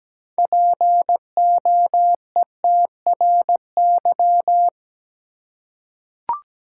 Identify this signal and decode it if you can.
Morse code, heard as a keyed tone, about 17 wpm, 700 Hz. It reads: POETRY
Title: Six Letter Words - 17wpm